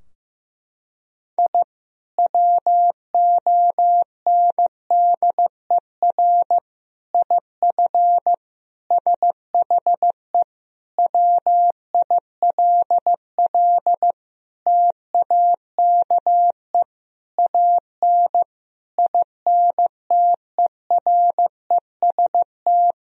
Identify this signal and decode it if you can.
Morse code, heard as a keyed tone, about 16 wpm, 700 Hz. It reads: I WONDER IF SHE WILL TAKE AN INTEREST
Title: Sentences from Top 1000 Words - Review Part 3 - 15wpm